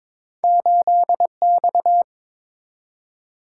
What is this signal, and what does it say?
Morse code, heard as a keyed tone, about 22 wpm, 700 Hz. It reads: 8X
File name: practice / 22wpm / Two Letter-Number Combinations - 22wpm